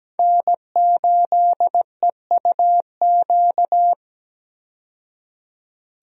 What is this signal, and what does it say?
Morse code, heard as a keyed tone, about 17 wpm, 700 Hz. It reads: N8EUQ
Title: USA Call Signs - 5 Characters - 17wpm